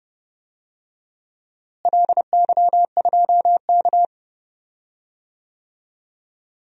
Morse code, heard as a keyed tone, about 30 wpm, 700 Hz. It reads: LY2K
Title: DXCC-100 Call Signs - 4 Characters - Repeated 3x - 30wpm